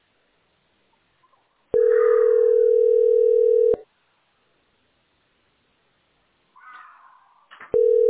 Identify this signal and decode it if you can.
Ringback tone (North America), 440 + 480 Hz.
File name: f86a59d1-3aa2-4ecd-b047